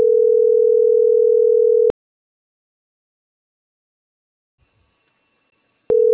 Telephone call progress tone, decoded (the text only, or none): ringback tone